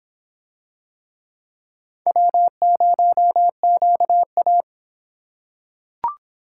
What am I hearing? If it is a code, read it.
Morse code, heard as a keyed tone, about 26 wpm, 700 Hz. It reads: W0QA